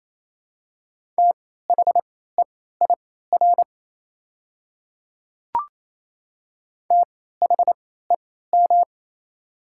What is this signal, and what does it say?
Morse code, heard as a keyed tone, about 28 wpm, spaced as for 16 wpm, 700 Hz. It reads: THEIR THEM
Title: 3x Character Spacing - Top 200 Words - 28wpm